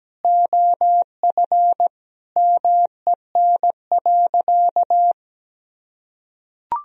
Morse code, heard as a keyed tone, about 17 wpm, 700 Hz. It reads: OF MEN.
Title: Sentences from Top 1000 Words - Review Part 1 - 17wpm